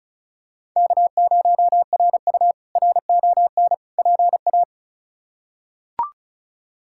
Morse code, heard as a keyed tone, about 36 wpm, 700 Hz. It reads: K0RU RON PA
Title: K1USN SST QSO Practice - 35wpm